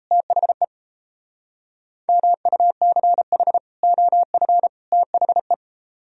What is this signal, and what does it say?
Morse code, heard as a keyed tone, about 35 wpm, 700 Hz. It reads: THE MUCH OF THE
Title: Speed-Racing - Sets of 3 Words 28wpm